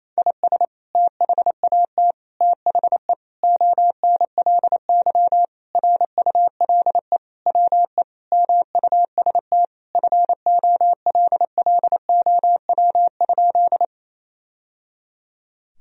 Morse code, heard as a keyed tone, about 27 wpm, 700 Hz. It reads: IS THAT THE ONLY RULE WE MUST FOLLOW?
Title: Sentences from Top 1000 Words - Review Part 2 - 28wpm